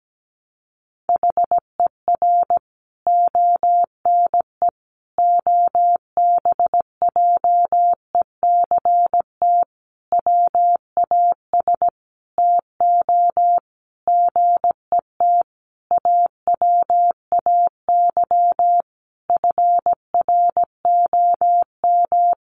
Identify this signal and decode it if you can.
Morse code, heard as a keyed tone, about 17 wpm, 700 Hz. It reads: HER ONE OBJECT WAS TO GET AWAY FROM
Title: Sentences from Top 1500 Words 17wpm